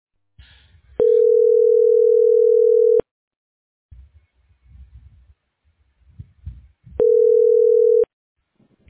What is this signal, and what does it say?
Ringback tone (North America), 440 + 480 Hz.